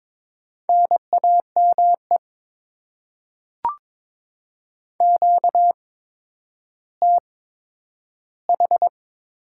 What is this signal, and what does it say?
Morse code, heard as a keyed tone, about 21 wpm, spaced as for 11 wpm, 700 Hz. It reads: NAME QTH
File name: Top 20 QSO Elements - 8x Character Spacing - 22wpm